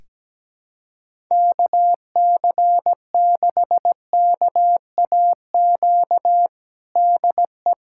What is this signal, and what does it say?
Morse code, heard as a keyed tone, about 17 wpm, 700 Hz. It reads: KC6KAQ DE
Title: USA Callsign DE Callsign - 6 Characters - 17wpm